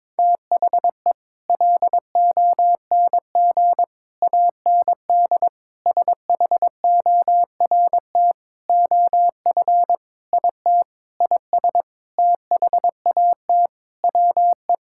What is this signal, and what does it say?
Morse code, heard as a keyed tone, about 22 wpm, 700 Hz. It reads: THE LONG AND SHORT OF IT IS THAT WE
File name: practice / 22wpm / Sentences with 8-Word Phrases - 22wpm